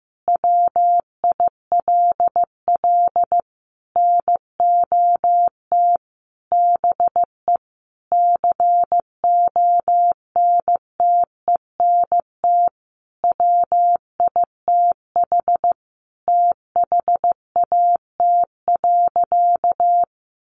Morse code, heard as a keyed tone, about 15 wpm, 700 Hz. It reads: WILL NOT BE CONTENT WITH THAT.
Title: Sentences from Top 900 Words 15wpm